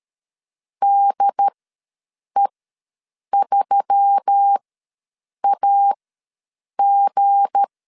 Morse code, heard as a keyed tone, about 13 wpm, spaced as for 7 wpm, 800 Hz. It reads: DE3AG
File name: LEZIONI / 7 GF9Z / 7 G ws 12wpm 3140